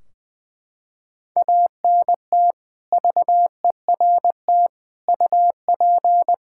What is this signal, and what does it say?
Morse code, heard as a keyed tone, about 20 wpm, 700 Hz. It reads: ANT VERT UP
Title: Practice QSO - Exchange 2 - Info - 20wpm